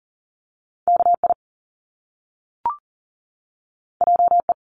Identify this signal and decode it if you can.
Morse code, heard as a keyed tone, about 40 wpm, 700 Hz. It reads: KI JE
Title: Top Two Letter Combinations in Words 40wpm